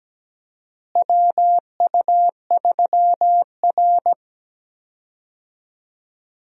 Morse code, heard as a keyed tone, about 17 wpm, 700 Hz. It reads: WU3R
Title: USA Call Signs - 4 Characters - Repeated 2x - 17wpm